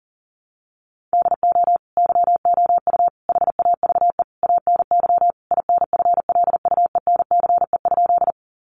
Morse code, heard as a keyed tone, about 40 wpm, 700 Hz. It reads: DO YOU HAVE ANY INFLUENCE?
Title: Sentences from Top 1100 Words 40wpm